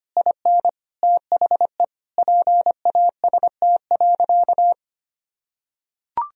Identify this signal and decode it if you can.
Morse code, heard as a keyed tone, about 25 wpm, 700 Hz. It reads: IN THE PAST.
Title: Sentences from Top 1000 Words - Review Part 3 - 25wpm